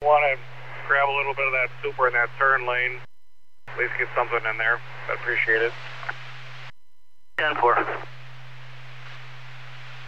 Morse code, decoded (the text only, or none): none